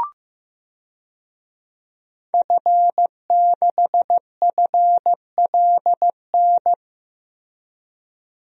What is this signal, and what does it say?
Morse code, heard as a keyed tone, about 15 wpm, 700 Hz. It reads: F6FLN
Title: DXCC-100 Call Signs - 5 Characters - Repeated 3x - 15wpm